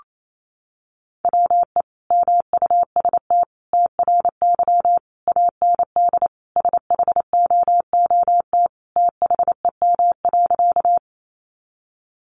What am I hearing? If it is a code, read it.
Morse code, heard as a keyed tone, about 28 wpm, 700 Hz. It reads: WE MUST TRY AND SHOOT THEM.
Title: Sentences from Top 1200 Words 28wpm